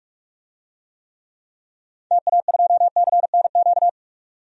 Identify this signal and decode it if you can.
Morse code, heard as a keyed tone, about 46 wpm, 700 Hz. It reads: TA2CNQ
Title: DXCC-100 Call Signs - 6 Characters - 45wpm